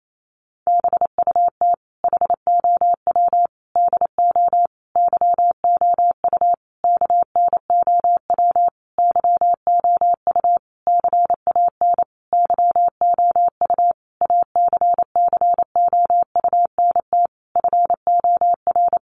Morse code, heard as a keyed tone, about 28 wpm, 700 Hz. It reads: BUT HOW DO YOU KNOW YOU CAN YOU ACCOUNT FOR
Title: Sentences from Top 900 Words 28wpm